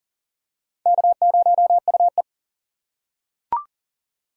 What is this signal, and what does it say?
Morse code, heard as a keyed tone, about 40 wpm, 700 Hz. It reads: K0UE
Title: USA Call Signs - 4 Characters - 3x Character Spacing - 40wpm